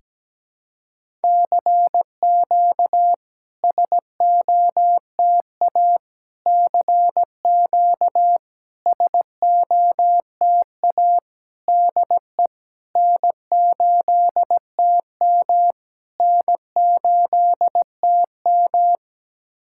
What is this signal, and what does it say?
Morse code, heard as a keyed tone, about 17 wpm, 700 Hz. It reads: CQ SOTA CQ SOTA DE N8TM N8TM
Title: SOTA QSO Practice 17wpm